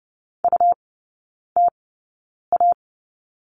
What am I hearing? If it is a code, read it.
Morse code, heard as a keyed tone, about 30 wpm, spaced as for 9 wpm, 700 Hz. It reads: UTA
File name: Lesson 17e - Letter u - Three Characters 30wpm